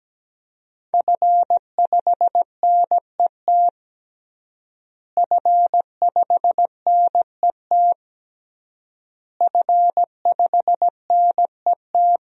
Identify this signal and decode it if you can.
Morse code, heard as a keyed tone, about 17 wpm, 700 Hz. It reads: F5NET F5NET F5NET
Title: DXCC-100 Call Signs - 5 Characters - Repeated 3x - 17wpm